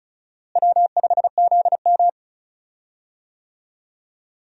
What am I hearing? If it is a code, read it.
Morse code, heard as a keyed tone, about 36 wpm, 700 Hz. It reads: W5ZM
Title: USA Call Signs - 4 Characters - Repeated 2x - 35wpm